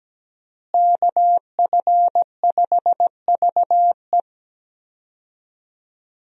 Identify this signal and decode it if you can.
Morse code, heard as a keyed tone, about 17 wpm, 700 Hz. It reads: KF5VE
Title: USA Call Signs - 5 Characters - Repeated 2x - 17wpm